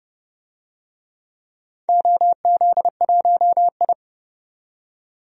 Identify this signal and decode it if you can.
Morse code, heard as a keyed tone, about 30 wpm, 700 Hz. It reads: OZ1I
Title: DXCC-100 Call Signs - 4 Characters - 30wpm